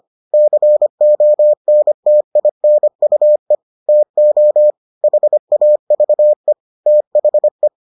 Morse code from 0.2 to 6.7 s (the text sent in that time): CONTINUE TO HAVE